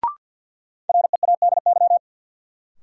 Morse code, heard as a keyed tone, about 50 wpm, 700 Hz. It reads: READY
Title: Sentences from Top 400 Words - New Word Drill - 50wpm